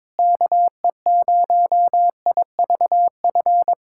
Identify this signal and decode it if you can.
Morse code, heard as a keyed tone, about 22 wpm, 700 Hz. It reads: KE0IVF
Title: USA Callsign DE Callsign - 6 Characters - 22wpm